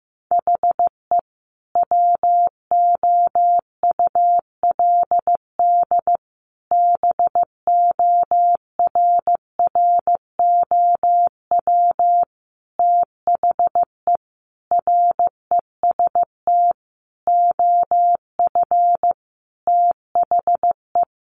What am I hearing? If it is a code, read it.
Morse code, heard as a keyed tone, about 15 wpm, 700 Hz. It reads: HE WOULD BORROW THE REST OF THE
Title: Sentences from Top 1500 Words 15wpm